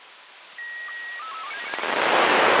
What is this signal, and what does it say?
SSTV (slow-scan television), Robot 36.
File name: arissat-aug05-2011